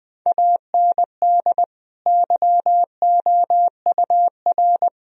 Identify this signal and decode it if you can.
Morse code, heard as a keyed tone, about 21 wpm, 700 Hz. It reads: AND YOUR
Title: Sentences from Top 1000 Words - Review Part 4 - 20wpm